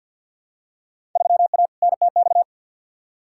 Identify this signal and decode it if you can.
Morse code, heard as a keyed tone, about 49 wpm, 700 Hz. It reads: 3A NTX
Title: Field Day Exchanges - 50wpm